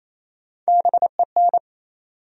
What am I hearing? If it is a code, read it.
Morse code, heard as a keyed tone, about 28 wpm, 700 Hz. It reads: BEN
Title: USA Common Names - Max 3 Letters - 28wpm